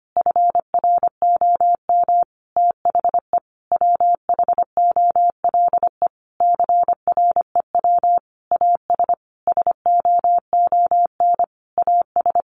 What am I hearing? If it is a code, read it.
Morse code, heard as a keyed tone, about 25 wpm, 700 Hz. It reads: FROM THE WHOLE CREW AS SOON AS
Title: Sentences from Top 1500 Words 25wpm